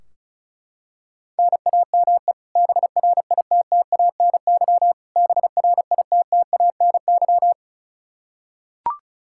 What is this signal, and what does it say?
Morse code, heard as a keyed tone, about 36 wpm, 700 Hz. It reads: NAME BRITTANY BRITTANY
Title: Practice QSO - Exchange 1 - Beginning - 35wpm